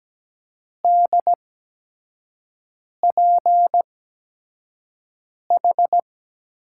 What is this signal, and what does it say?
Morse code, heard as a keyed tone, about 17 wpm, spaced as for 5 wpm, 700 Hz. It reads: DPH
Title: USA Call Signs - 6 Characters - 8x Character Spacing - 17wpm